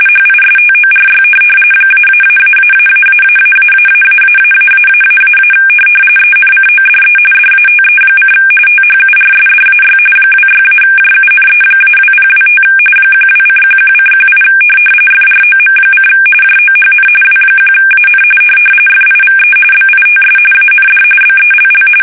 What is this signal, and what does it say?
RTTY, about 45.45 baud. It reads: HISTORY ---------------------- ENCODE DATE: THURSDAY 26-JUNE-2025 12:11:18 FROM: JOHN W1AN FIVE BITS AND BELL